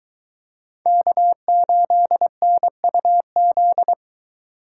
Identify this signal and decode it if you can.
Morse code, heard as a keyed tone, about 23 wpm, 700 Hz. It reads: K8NUZ